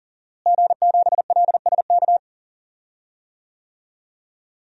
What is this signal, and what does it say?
Morse code, heard as a keyed tone, about 40 wpm, 700 Hz. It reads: G7LSK
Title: DXCC-100 Call Signs - 5 Characters - Repeated 2x - 40wpm